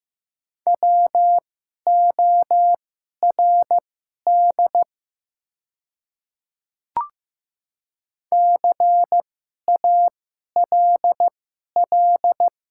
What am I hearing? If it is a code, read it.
Morse code, heard as a keyed tone, about 15 wpm, spaced as for 11 wpm, 700 Hz. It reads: WORD CALL
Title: 2x Character Spacing - Top 100 Words - 15wpm